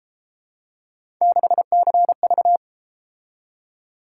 Morse code, heard as a keyed tone, about 33 wpm, 700 Hz. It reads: 6CV